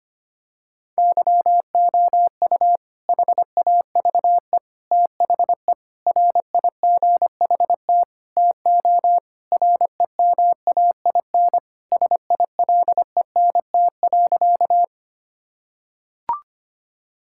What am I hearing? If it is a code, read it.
Morse code, heard as a keyed tone, about 25 wpm, 700 Hz. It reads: YOU HAVE THE RIGHT TO REMAIN SILENT.
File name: practice / 25wpm / Sentences with 4-Word Phrases - 25wpm